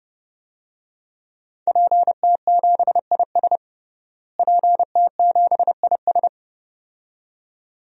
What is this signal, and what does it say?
Morse code, heard as a keyed tone, about 30 wpm, 700 Hz. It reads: PT7IS PT7IS
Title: DXCC-100 Call Signs - 5 Characters - Repeated 2x - 30wpm